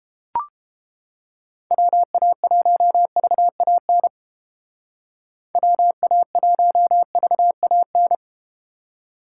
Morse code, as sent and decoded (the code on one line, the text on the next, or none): .-- .- .---- ...- .- -. / .-- .- .---- ...- .- -.
WA1VAN WA1VAN